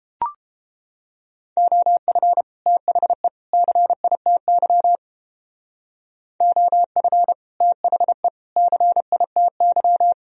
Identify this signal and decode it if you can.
Morse code, heard as a keyed tone, about 29 wpm, 700 Hz. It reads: OF THE CITY OF THE CITY